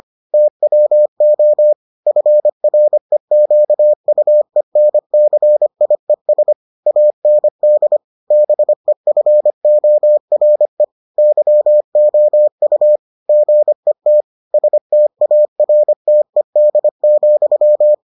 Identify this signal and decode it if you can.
Morse code, heard as a keyed tone, about 25 wpm, 600 Hz. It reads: TWO FREQUENCIES AND BEFORE YOU GET STARTED,